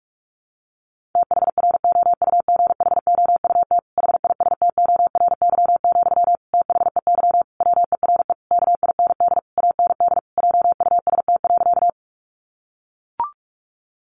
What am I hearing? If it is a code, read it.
Morse code, heard as a keyed tone, about 44 wpm, 700 Hz. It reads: THROUGHOUT HISTORY, THEY WERE KIND AND JUST.
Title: Sentences from Top 800 Words 45wpm